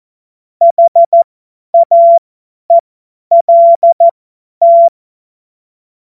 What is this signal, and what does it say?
Morse code, heard as a keyed tone, about 14 wpm, spaced as for 10 wpm, 680 Hz. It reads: HAELT
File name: Morsekurs / Witze / Witz 26. April 2022 50 F70